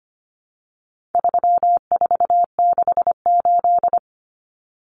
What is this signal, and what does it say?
Morse code, heard as a keyed tone, about 25 wpm, 700 Hz. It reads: 3468
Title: Numbers - 4-Digit 25wpm